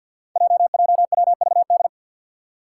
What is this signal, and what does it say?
Morse code, heard as a keyed tone, about 52 wpm, 700 Hz. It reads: JJWVD
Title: Code Groups - 5 Letters - 50wpm